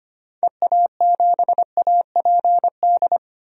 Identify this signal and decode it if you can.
Morse code, heard as a keyed tone, about 25 wpm, 700 Hz. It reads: EA7APD
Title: DXCC-100 Call Signs - 6 Characters - 25wpm